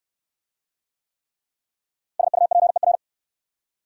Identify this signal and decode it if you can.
Morse code, heard as a keyed tone, about 66 wpm, 690 Hz. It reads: SUPER